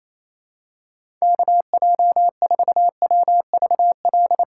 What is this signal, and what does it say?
Morse code, heard as a keyed tone, about 28 wpm, 690 Hz. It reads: KJ4WVL
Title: USA Call Signs - 6 Characters - 2x Character Spacing - 28wpm